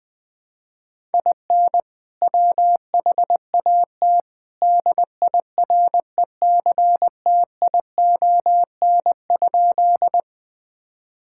Morse code, heard as a keyed tone, about 20 wpm, 700 Hz. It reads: IN WHAT DIRECTION?
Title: Sentences from Top 1000 Words - Review Part 5 - 20wpm